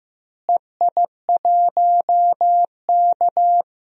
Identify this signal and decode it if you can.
Morse code, heard as a keyed tone, about 15 wpm, 700 Hz. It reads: EI1K
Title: DXCC-100 Call Signs - 4 Characters - Repeated 2x - 15wpm